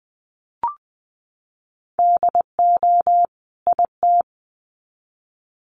Morse code, heard as a keyed tone, about 20 wpm, 700 Hz. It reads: DO IT
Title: Sets of 2 Words 20wpm